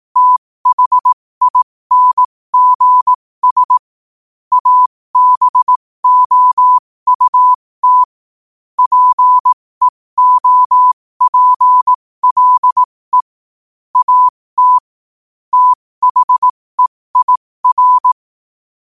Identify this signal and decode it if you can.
Morse code, heard as a keyed tone, about 18 wpm, spaced as for 15 wpm, 1000 Hz. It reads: THINGS ABOUT PEOPLE AT THEIR
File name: qotdfiles / 15wpm